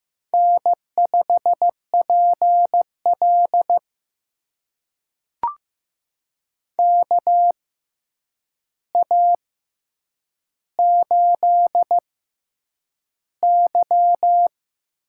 Morse code, heard as a keyed tone, about 15 wpm, spaced as for 8 wpm, 700 Hz. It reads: N5PL KA8Y
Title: USA Call Signs - 6x Character Spacing - 15wpm